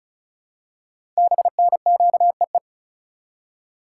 Morse code, heard as a keyed tone, about 36 wpm, 690 Hz. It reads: BNQEE
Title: Code Groups - 5 Letters - 35wpm